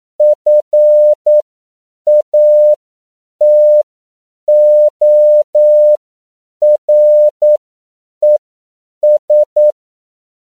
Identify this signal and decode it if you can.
Morse code, heard as a keyed tone, about 9 wpm, spaced as for 7 wpm, 600 Hz. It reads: FATORES
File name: CW / 10PPM